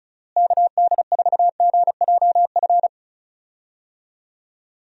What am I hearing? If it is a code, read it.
Morse code, heard as a keyed tone, about 34 wpm, 700 Hz. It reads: KD4GJF
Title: USA Call Signs - 6 Characters - Repeated 3x - 35wpm